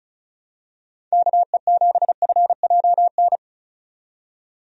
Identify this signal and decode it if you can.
Morse code, heard as a keyed tone, about 36 wpm, 700 Hz. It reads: KE7FJN
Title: USA Call Signs - 6 Characters - 35wpm